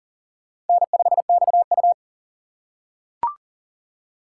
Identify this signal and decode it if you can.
Morse code, heard as a keyed tone, about 40 wpm, 700 Hz. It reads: N5XU
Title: USA Call Signs - 4 Characters - 3x Character Spacing - 40wpm